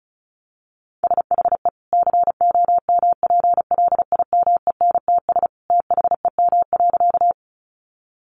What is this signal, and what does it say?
Morse code, heard as a keyed tone, about 34 wpm, 700 Hz. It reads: SHE COMPLIMENTS THEM.
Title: Easy Phrases 35wpm